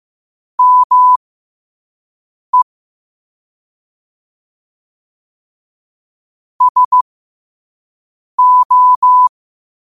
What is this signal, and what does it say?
Morse code, heard as a keyed tone, about 15 wpm, spaced as for 5 wpm, 1000 Hz. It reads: ME SO
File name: qotdfiles / 5wpm